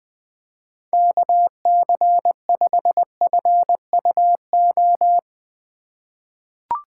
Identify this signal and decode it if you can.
Morse code, heard as a keyed tone, about 20 wpm, 700 Hz. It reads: KC5FUO
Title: USA Call Signs - 4x Character Spacing - 20wpm